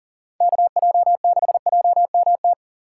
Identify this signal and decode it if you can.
Morse code, heard as a keyed tone, about 38 wpm, 700 Hz. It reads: KJ6JMT
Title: USA Call Signs - 6 Characters - 2x Character Spacing - 40wpm